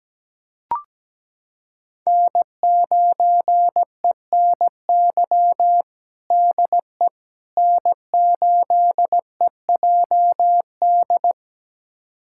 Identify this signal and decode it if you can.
Morse code, heard as a keyed tone, about 17 wpm, 700 Hz. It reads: N9ENY DE N8EJD
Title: USA Callsign DE Callsign - 5 Characters - 17wpm